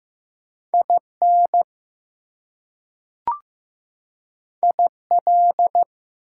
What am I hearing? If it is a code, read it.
Morse code, heard as a keyed tone, about 15 wpm, 700 Hz. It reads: IN IL
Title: US States and Canadian Provinces 15wpm